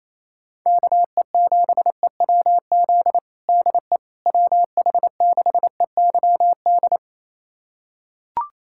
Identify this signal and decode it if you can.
Morse code, heard as a keyed tone, about 28 wpm, 700 Hz. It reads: KE7EWZ DE WH6EYD
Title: USA Callsign DE Callsign 28wpm